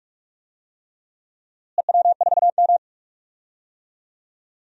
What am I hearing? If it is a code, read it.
Morse code, heard as a keyed tone, about 45 wpm, 700 Hz. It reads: EW4M
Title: DXCC-100 Call Signs - 4 Characters - Repeated 3x - 45wpm